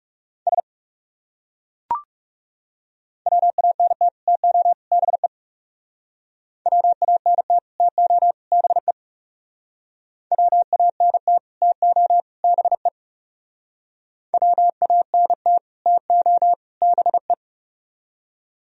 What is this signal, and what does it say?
Morse code, heard as a keyed tone, about 36 wpm, 690 Hz. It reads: S WANT TO BE WANT TO BE WANT TO BE WANT TO BE
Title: Speed-Racing - Sets of 3 Words 30wpm